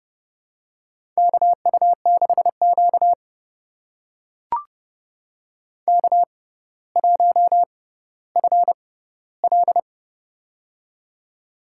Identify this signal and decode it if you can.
Morse code, heard as a keyed tone, about 31 wpm, spaced as for 15 wpm, 700 Hz. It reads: KU6Q K1FL
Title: USA Call Signs - 4 Characters - 6x Character Spacing - 30wpm